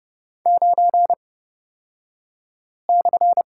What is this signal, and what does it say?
Morse code, heard as a keyed tone, about 30 wpm, spaced as for 5 wpm, 700 Hz. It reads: 9/
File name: Lesson 36d - BT - Two Characters 30wpm